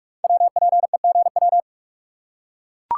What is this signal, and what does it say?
Morse code, heard as a keyed tone, about 44 wpm, 700 Hz. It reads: WPEGW
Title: Code Groups - 5 Letter:Numbers - 45wpm